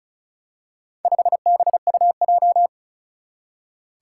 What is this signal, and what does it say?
Morse code, heard as a keyed tone, about 36 wpm, 700 Hz. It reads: 5BUJ
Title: Four Challenging Letter-Number Combinations 35wpm